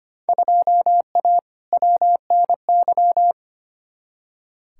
Morse code, heard as a keyed tone, about 25 wpm, 700 Hz. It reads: 2A WNY
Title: Field Day Exchanges - 25wpm